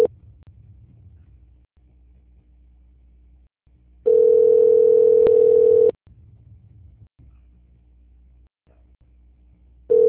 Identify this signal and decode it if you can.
Ringback tone (North America), 440 + 480 Hz.
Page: Bangu - RJ